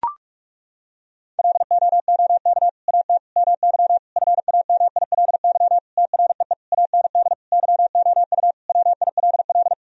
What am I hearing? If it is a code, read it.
Morse code, heard as a keyed tone, about 41 wpm, 700 Hz. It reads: LOOK AT MY FAMILY TREE AND YOU WILL
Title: Sentences from Top 1000 Words - Review Part 1 - 45wpm